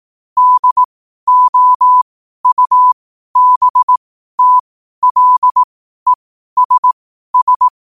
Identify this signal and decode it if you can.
Morse code, heard as a keyed tone, about 18 wpm, spaced as for 12 wpm, 1000 Hz. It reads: DOUBTLESS
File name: qotdfiles / 12wpm